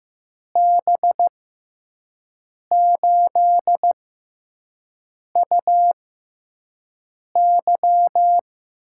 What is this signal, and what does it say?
Morse code, heard as a keyed tone, about 15 wpm, spaced as for 5 wpm, 700 Hz. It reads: B8UY
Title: USA Call Signs - 6 Characters - 6x Character Spacing - 15wpm